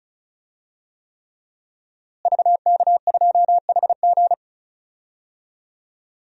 Morse code, heard as a keyed tone, about 34 wpm, 700 Hz. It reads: VK2HG
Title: DXCC-100 Call Signs - 5 Characters - Repeated 3x - 35wpm